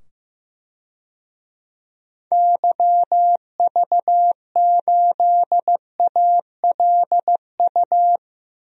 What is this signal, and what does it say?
Morse code, heard as a keyed tone, about 15 wpm, 700 Hz. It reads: YV8ALU
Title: DXCC-100 Call Signs - 6 Characters - 15wpm